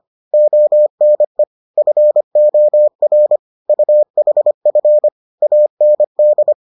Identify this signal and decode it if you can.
Morse code, heard as a keyed tone, about 25 wpm, 600 Hz. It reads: ONE FOR UHF AND